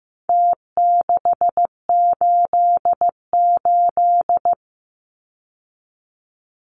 Morse code, heard as a keyed tone, about 15 wpm, 700 Hz. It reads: T688
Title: Numbers - Contest Style - 4-Digit 15wpm